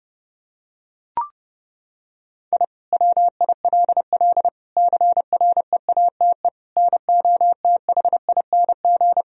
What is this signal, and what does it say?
Morse code, heard as a keyed tone, about 30 wpm, 700 Hz. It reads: I WILL CREATE NOTHING
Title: Sentences from Top 1000 Words - Review Part 3 - 30wpm